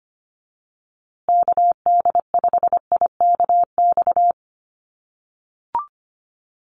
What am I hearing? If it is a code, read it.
Morse code, heard as a keyed tone, about 25 wpm, 700 Hz. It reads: KD5IKX
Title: USA Call Signs Repeated 2x 25wpm